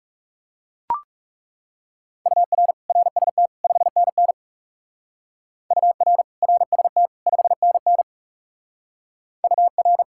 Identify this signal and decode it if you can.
Morse code, heard as a keyed tone, about 40 wpm, 700 Hz. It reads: UR RST 5NN UR RST 5NN UR